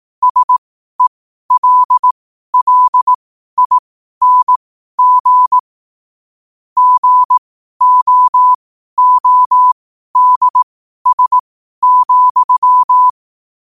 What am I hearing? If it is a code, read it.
Morse code, heard as a keyed tone, about 18 wpm, spaced as for 12 wpm, 1000 Hz. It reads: SELLING GOODS,
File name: qotdfiles / 12wpm